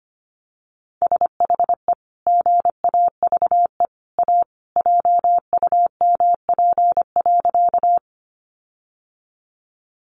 Morse code, heard as a keyed tone, about 25 wpm, 700 Hz. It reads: SHE GAVE A JUMP.
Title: Sentences from Top 1300 Words 25wpm